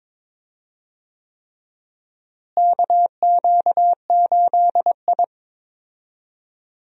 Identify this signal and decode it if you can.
Morse code, heard as a keyed tone, about 21 wpm, 700 Hz. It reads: KQ8I
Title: USA Call Signs - 4 Characters - Repeated 3x - 22wpm